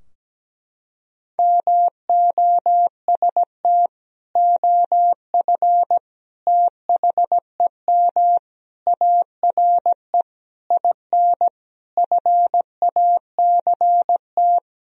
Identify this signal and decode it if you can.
Morse code, heard as a keyed tone, about 17 wpm, 700 Hz. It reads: MOST OF THEM ARE IN FACT
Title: Sentences from Top 1000 Words - Review Part 4 - 17wpm